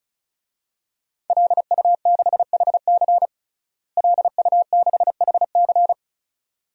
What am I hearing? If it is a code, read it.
Morse code, heard as a keyed tone, about 36 wpm, 700 Hz. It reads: LU6HC LU6HC
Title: DXCC-100 Call Signs - 5 Characters - Repeated 2x - 35wpm